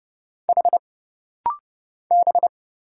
Morse code, heard as a keyed tone, about 30 wpm, 700 Hz. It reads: H B
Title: Single Letters - Rapid-Fire 30wpm